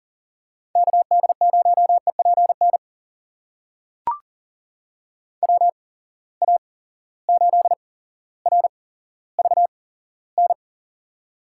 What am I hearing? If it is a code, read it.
Morse code, heard as a keyed tone, about 40 wpm, spaced as for 17 wpm, 700 Hz. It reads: KD0EPN WA8RVN
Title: USA Call Signs - 6 Characters - 8x Character Spacing - 40wpm